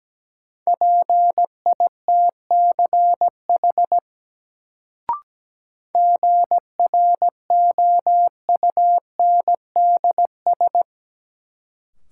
Morse code, heard as a keyed tone, about 17 wpm, 700 Hz. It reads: PITCH GROUNDS
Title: Sentences from Top 1300 Words - New Word Drill Alt - 17wpm